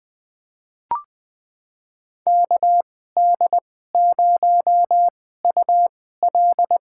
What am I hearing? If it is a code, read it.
Morse code, heard as a keyed tone, about 20 wpm, spaced as for 14 wpm, 700 Hz. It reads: KD0UL
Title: USA Call Signs - 2x Character Spacing - 20wpm